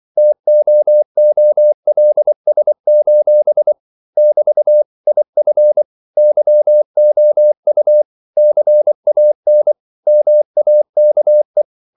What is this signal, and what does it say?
Morse code, heard as a keyed tone, about 24 wpm, 600 Hz. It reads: TOOLS: <BT> IF YOU CAN MAKE
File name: CW / Practice / If24